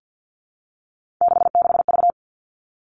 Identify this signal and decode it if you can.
Morse code, heard as a keyed tone, about 50 wpm, 700 Hz. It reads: 66V
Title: Three Challenging Letter-Number Combinations 50wpm